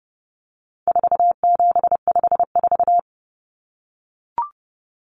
Morse code, heard as a keyed tone, about 30 wpm, 700 Hz. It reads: 4754
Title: Common Cut Numbers - 4-Digit 30wpm